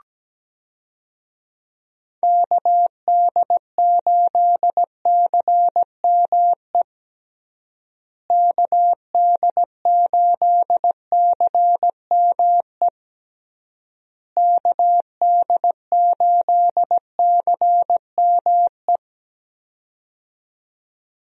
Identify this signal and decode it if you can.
Morse code, heard as a keyed tone, about 17 wpm, 700 Hz. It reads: KD8CME KD8CME KD8CME
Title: USA Call Signs - 6 Characters - Repeated 3x - 17wpm